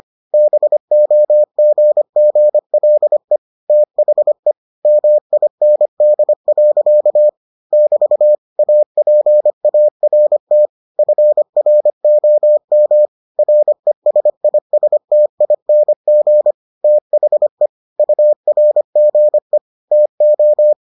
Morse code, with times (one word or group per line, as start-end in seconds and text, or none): 0.3-3.4 BOGGLE
3.7-4.5 THE
4.9-7.3 MIND.
7.7-8.4 <BT>
8.6-10.7 APART
11.0-13.1 FROM
13.4-16.5 RESISTING
16.8-17.7 THE
18.0-19.6 URGE
19.9-20.7 TO